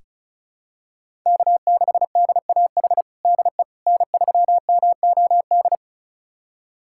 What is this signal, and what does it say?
Morse code, heard as a keyed tone, about 36 wpm, 700 Hz. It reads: K6DAH DE N3MOD
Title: USA Callsign DE Callsign - 5 Characters - 35wpm